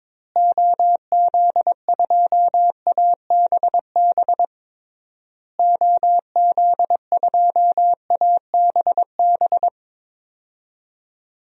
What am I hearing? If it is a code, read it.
Morse code, heard as a keyed tone, about 22 wpm, 700 Hz. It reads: OZ2ABB OZ2ABB
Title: DXCC-100 Call Signs - 6 Characters - Repeated 3x - 22wpm